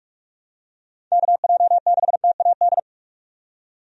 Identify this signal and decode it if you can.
Morse code, heard as a keyed tone, about 44 wpm, 700 Hz. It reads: KJ6TAD
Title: USA Call Signs - 6 Characters - 45wpm